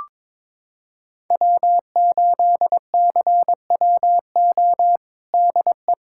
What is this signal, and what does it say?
Morse code, heard as a keyed tone, about 22 wpm, 700 Hz. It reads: W8CWO DE
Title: USA Callsign DE Callsign - 5 Characters - 22wpm